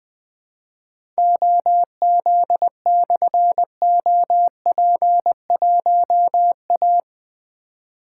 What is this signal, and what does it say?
Morse code, heard as a keyed tone, about 20 wpm, 700 Hz. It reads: OZ/OP1A
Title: Difficult Call Signs 20wpm